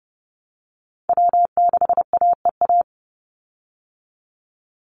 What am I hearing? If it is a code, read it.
Morse code, heard as a keyed tone, about 30 wpm, 700 Hz. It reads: W6AEA
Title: Call Signs Repeated 2x - 30wpm